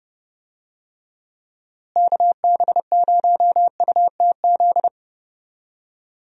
Morse code, heard as a keyed tone, about 30 wpm, 700 Hz. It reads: KB0UTZ
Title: USA Call Signs - 6 Characters - Repeated 3x - 30wpm